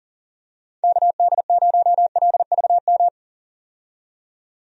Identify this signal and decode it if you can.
Morse code, heard as a keyed tone, about 39 wpm, 700 Hz. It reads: KD0LVM
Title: USA Call Signs - 6 Characters - 40wpm